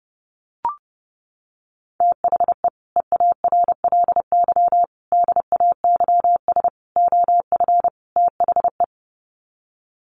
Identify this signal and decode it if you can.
Morse code, heard as a keyed tone, about 30 wpm, 700 Hz. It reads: THE EARLY DAYS OF THE
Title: Sets of 5 Words Encore 30wpm